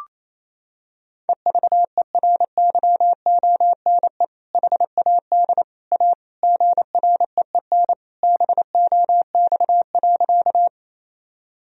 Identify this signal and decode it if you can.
Morse code, heard as a keyed tone, about 28 wpm, 700 Hz. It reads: EVERYONE HAD A GREEN BOX.
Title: Sentences from Top 1000 Words - Review Part 2 - 28wpm